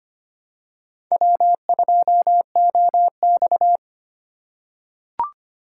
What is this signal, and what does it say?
Morse code, heard as a keyed tone, about 25 wpm, 700 Hz. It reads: W2OX
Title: USA Call Signs - 3x Character Spacing - 25wpm